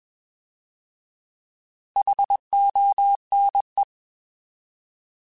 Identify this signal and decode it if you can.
Morse code, heard as a keyed tone, about 21 wpm, 800 Hz. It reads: HONE
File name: Alphabetical / Alphabetical mp3s / Hs @20